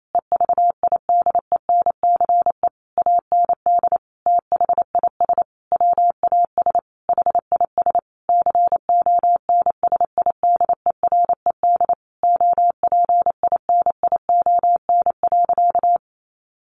Morse code, heard as a keyed tone, about 28 wpm, 700 Hz. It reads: EVIDENCE AND THIS WAS HIS CONSIDERED OPINION.
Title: A Sound Of Thunder 28wpm